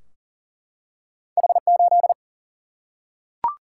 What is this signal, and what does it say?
Morse code, heard as a keyed tone, about 40 wpm, 700 Hz. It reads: H8
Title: Single-Letter Single-Number 40wpm